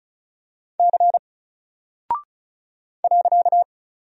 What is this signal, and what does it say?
Morse code, heard as a keyed tone, about 36 wpm, 700 Hz. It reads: C .
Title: Single Character (Letter:Number:Punctuation:Prosign) Alt 35wpm